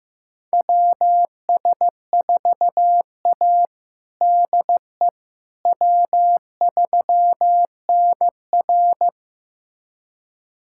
Morse code, heard as a keyed tone, about 16 wpm, 700 Hz. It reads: WS4A DE W3NR
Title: USA Callsign DE Callsign - 4 Characters - 15wpm